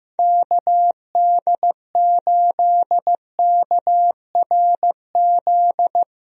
Morse code, heard as a keyed tone, about 15 wpm, 700 Hz. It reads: KD8KRZ
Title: USA Call Signs - 6 Characters - Repeated 2x - 15wpm